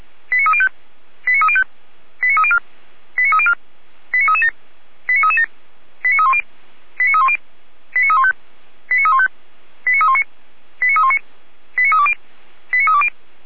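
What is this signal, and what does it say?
Five-tone selective call (ZVEI), heard as quick runs of five tones, tones of about 80 ms. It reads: 89305 89305 89304 89304 89337 89337 89310 89310 89315 89315 89319 89319 89320 89320
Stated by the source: Five-tone ZVEI selective call received by SDR, from the feuerwehr-orlamuende.org (F85360) audio server.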